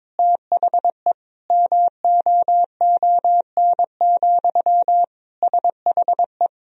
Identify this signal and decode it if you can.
Morse code, heard as a keyed tone, about 22 wpm, 700 Hz. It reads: THE MOON, SHE
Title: Sentences from Top 1000 Words - Review Part 3 - 22wpm